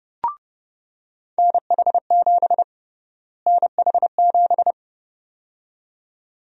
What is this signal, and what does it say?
Morse code, heard as a keyed tone, about 30 wpm, 700 Hz. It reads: NH7 NH7
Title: USA Call Sign Prefixes - Repeated 2x - 30wpm